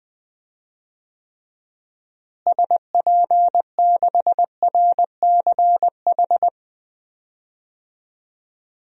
Morse code, heard as a keyed tone, about 20 wpm, 700 Hz. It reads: SP6RCH
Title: DXCC-100 Call Signs - 6 Characters - Repeated 3x - 20wpm